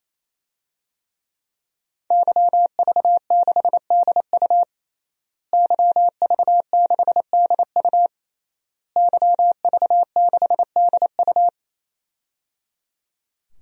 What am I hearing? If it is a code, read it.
Morse code, heard as a keyed tone, about 28 wpm, 700 Hz. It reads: YV6DU YV6DU YV6DU
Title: DXCC-100 Call Signs - 5 Characters - Repeated 3x - 28wpm